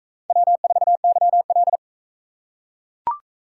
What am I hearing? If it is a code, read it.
Morse code, heard as a keyed tone, about 44 wpm, 700 Hz. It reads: W4YL